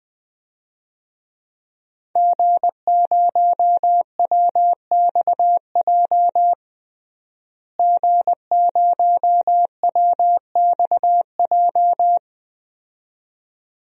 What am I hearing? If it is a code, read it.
Morse code, heard as a keyed tone, about 20 wpm, 700 Hz. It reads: G0WXJ G0WXJ
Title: DXCC-100 Call Signs - 5 Characters - Repeated 2x - 20wpm